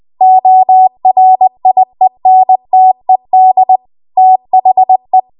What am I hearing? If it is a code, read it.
Morse code, heard as a keyed tone, about 20 wpm, 750 Hz. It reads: ORIENTED THE
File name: Archive / 20 WPM / 20WPM